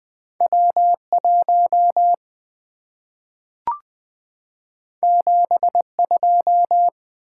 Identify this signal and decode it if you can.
Morse code, heard as a keyed tone, about 20 wpm, 690 Hz. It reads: W1 72
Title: Two Letter-Number Combinations - 20wpm